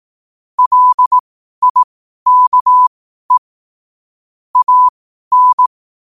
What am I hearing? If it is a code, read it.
Morse code, heard as a keyed tone, about 18 wpm, spaced as for 12 wpm, 1000 Hz. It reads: LIKE AN
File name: qotdfiles / 12wpm